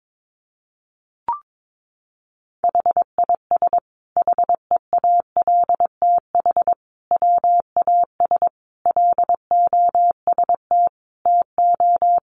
Morse code, heard as a keyed tone, about 22 wpm, 700 Hz. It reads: HIS HEALTH WAS LOST TO
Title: Sentences from Top 1400 Words 22wpm